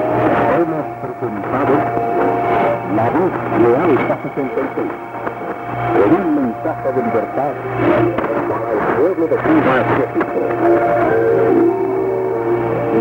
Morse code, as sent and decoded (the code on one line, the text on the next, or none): none